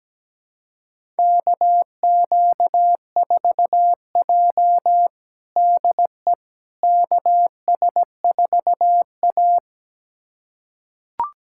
Morse code, heard as a keyed tone, about 17 wpm, 700 Hz. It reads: KQ4J DE KS4A
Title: USA Callsign DE Callsign - 4 Characters - 17wpm